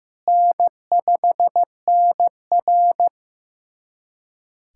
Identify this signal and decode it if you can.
Morse code, heard as a keyed tone, about 15 wpm, 700 Hz. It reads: N5NR
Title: USA Callsign DE Callsign - 4 Characters - 15wpm